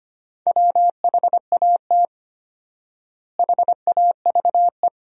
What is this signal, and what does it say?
Morse code, heard as a keyed tone, about 25 wpm, 700 Hz. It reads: WHAT HAVE
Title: 4x Word-Spacing - Sentences from Top 400 Words - 25wpm